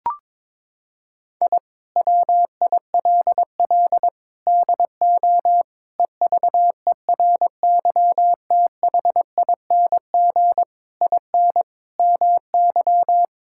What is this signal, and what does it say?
Morse code, heard as a keyed tone, about 22 wpm, 700 Hz. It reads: I WILL DO EVERYTHING IN MY
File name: practice / 22wpm / Sets of 8 Words - 22wpm